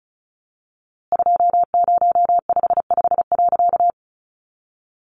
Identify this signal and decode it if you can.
Morse code, heard as a keyed tone, about 34 wpm, 700 Hz. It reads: 2055.
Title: A Sound Of Thunder 35wpm